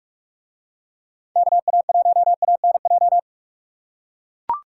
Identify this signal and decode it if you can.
Morse code, heard as a keyed tone, about 44 wpm, 700 Hz. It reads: KA1ANJ